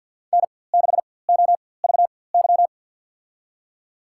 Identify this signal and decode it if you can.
Morse code, heard as a keyed tone, about 53 wpm, spaced as for 24 wpm, 700 Hz. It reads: N6OVY
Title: USA Call Signs - 5 Characters - 4x Character Spacing - 50wpm